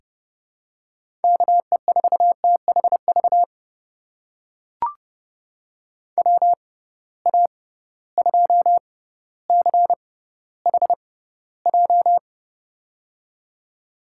Morse code, heard as a keyed tone, about 31 wpm, spaced as for 15 wpm, 700 Hz. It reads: KE4THV WA2CHJ
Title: USA Call Signs - 6 Characters - 6x Character Spacing - 30wpm